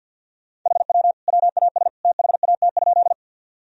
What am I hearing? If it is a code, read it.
Morse code, heard as a keyed tone, about 51 wpm, 700 Hz. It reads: HW WAS THAT?
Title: Sentences from Top 100 Words and Abbreviations 50wpm